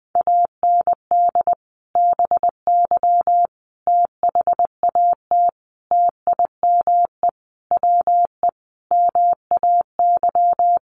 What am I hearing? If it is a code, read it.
Morse code, heard as a keyed tone, about 20 wpm, 700 Hz. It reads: AND BY THAT TIME WE MAY
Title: Sentences from Top 1100 Words 20wpm